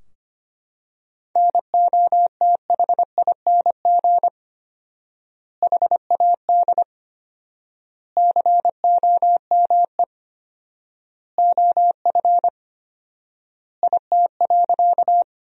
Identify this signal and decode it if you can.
Morse code, heard as a keyed tone, about 25 wpm, 700 Hz. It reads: NOTHING HAD COME OF IT.
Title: 4x Word-Spacing - Sentences from Top 500 Words - 25wpm